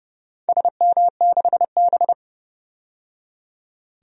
Morse code, heard as a keyed tone, about 29 wpm, 700 Hz. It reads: SM6B
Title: DXCC-100 Call Signs - 4 Characters - 30wpm